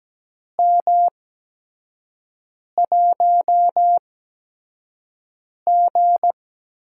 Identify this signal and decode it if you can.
Morse code, heard as a keyed tone, about 17 wpm, spaced as for 5 wpm, 700 Hz. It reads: M1G
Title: USA Call Signs - 4 Characters - 8x Character Spacing - 17wpm